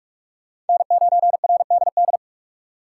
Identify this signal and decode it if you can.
Morse code, heard as a keyed tone, about 44 wpm, 700 Hz. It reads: N9RDD
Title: USA Call Signs - 5 Characters - 4x Character Spacing - 45wpm